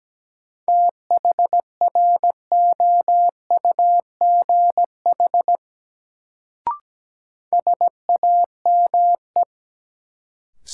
Morse code, heard as a keyed tone, about 17 wpm, 700 Hz. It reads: THROUGH SAME
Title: Sentences from Top 200 Words - New Word Drill - 17wpm